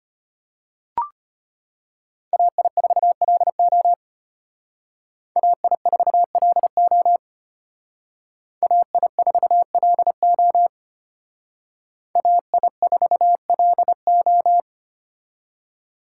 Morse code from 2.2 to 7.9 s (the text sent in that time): AI4LO AI4LO